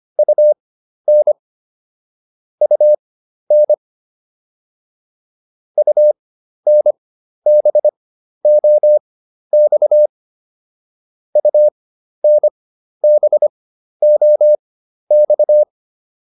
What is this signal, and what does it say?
Morse code, heard as a keyed tone, about 25 wpm, spaced as for 12 wpm, 600 Hz. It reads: UN UN UNBOX UNBOX